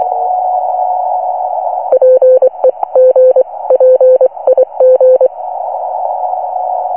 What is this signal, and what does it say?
Morse code, heard as a keyed tone, about 24 wpm, spaced as for 20 wpm, 510 Hz, with noise. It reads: PEGPIG